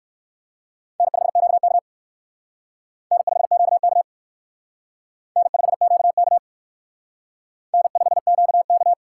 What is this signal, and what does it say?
Morse code, heard as a keyed tone, about 56 wpm, 700 Hz. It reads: N5QK N5QK N5QK N5QK
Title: Speed Racing - USA 1x2 Call Signs 45wpm